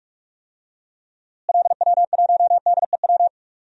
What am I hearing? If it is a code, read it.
Morse code, heard as a keyed tone, about 41 wpm, 700 Hz. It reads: LW1DEW
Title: DXCC-100 Call Signs - 6 Characters - Repeated 2x - 45wpm